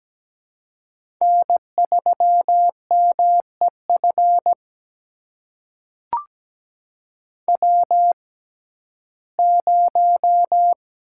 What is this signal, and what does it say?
Morse code, heard as a keyed tone, about 17 wpm, spaced as for 12 wpm, 700 Hz. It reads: N3MEF W0
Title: USA Call Signs - 5 Characters - 6x Character Spacing - 17wpm